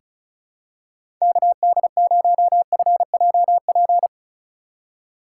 Morse code, heard as a keyed tone, about 34 wpm, 700 Hz. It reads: KD0FJP
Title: USA Call Signs - 6 Characters - 3x Character Spacing - 35wpm